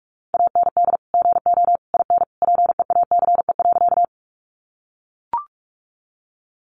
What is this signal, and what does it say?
Morse code, heard as a keyed tone, about 44 wpm, 700 Hz. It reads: AND GO IN PEACE.
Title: Sentences from Top 1500 Words 45wpm